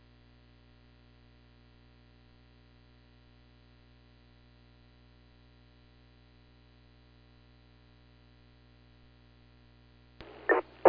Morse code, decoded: none

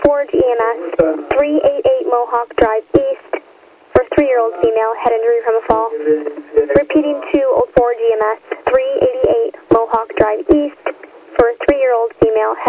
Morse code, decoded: none

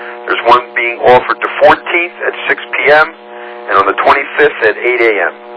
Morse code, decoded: none

